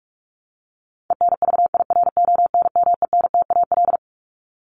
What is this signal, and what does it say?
Morse code, heard as a keyed tone, about 44 wpm, 700 Hz. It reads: ENVIRONMENTAL
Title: Top 1001-1250 Words 45wpm